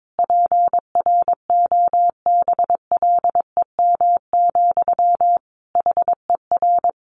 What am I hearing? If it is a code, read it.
Morse code, heard as a keyed tone, about 22 wpm, 700 Hz. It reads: PROBLEM, HER
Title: Sentences from Top 1200 Words 22wpm